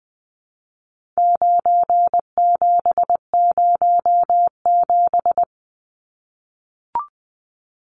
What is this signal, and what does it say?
Morse code, heard as a keyed tone, about 20 wpm, 700 Hz. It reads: 9707
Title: Numbers - 4-Digit 20wpm